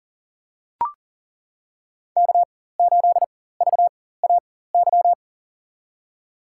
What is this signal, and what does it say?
Morse code, heard as a keyed tone, about 40 wpm, spaced as for 19 wpm, 700 Hz. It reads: K8VAY
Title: USA Call Signs - 5 Characters - 4x Character Spacing - 40wpm